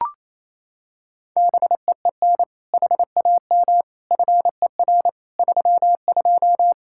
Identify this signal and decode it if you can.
Morse code, heard as a keyed tone, about 28 wpm, 700 Hz. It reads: BEEN HAM FER 32
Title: K1USN SST QSO Practice 28wpm